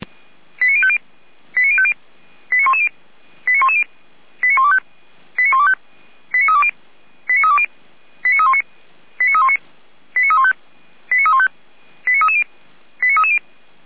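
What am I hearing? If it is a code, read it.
Five-tone selective call (ZVEI), heard as quick runs of five tones, tones of about 70 ms. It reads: 89050 89050 89110 89110 89125 89125 89320 89320 89319 89319 89315 89315 89330 89330
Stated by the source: Five-tone ZVEI selective call received by SDR.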